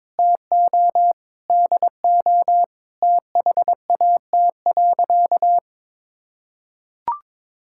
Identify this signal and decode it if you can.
Morse code, heard as a keyed tone, about 22 wpm, 700 Hz. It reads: TO DO THAT.
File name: practice / 22wpm / Sentences with 3-Word Phrases - 22wpm